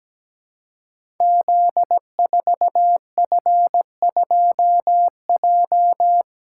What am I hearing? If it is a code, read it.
Morse code, heard as a keyed tone, about 17 wpm, 700 Hz. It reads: Z4F2J
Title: Code Groups - 5 Letter:Numbers - 17wpm